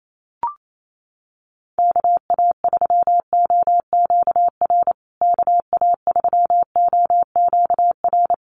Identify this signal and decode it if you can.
Morse code, heard as a keyed tone, about 28 wpm, 700 Hz. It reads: KA3OQR KA3OQR
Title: USA Call Signs Repeated 3x 28wpm